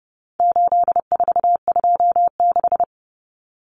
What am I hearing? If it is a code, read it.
Morse code, heard as a keyed tone, about 30 wpm, 700 Hz. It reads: 8426
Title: Numbers - 4-Digit 30wpm